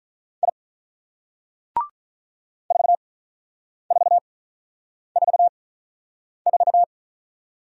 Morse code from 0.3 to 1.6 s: I

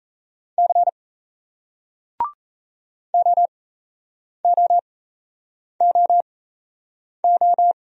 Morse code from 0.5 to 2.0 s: C

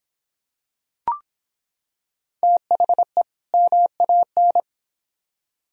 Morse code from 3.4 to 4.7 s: MAN

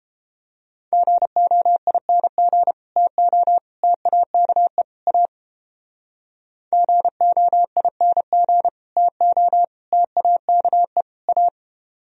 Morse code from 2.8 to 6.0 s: TO TAKE A

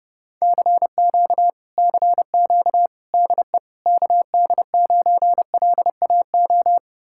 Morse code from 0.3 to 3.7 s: CQ CQ DE

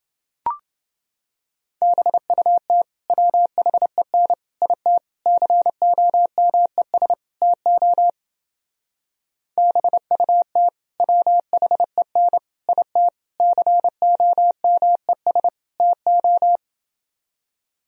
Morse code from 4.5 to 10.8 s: IT COMES TO BUT

